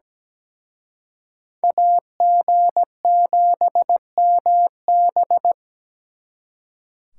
Morse code, heard as a keyed tone, about 20 wpm, 700 Hz. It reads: AG7MB
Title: Speed Racing - USA 2x2 Call Signs 17wpm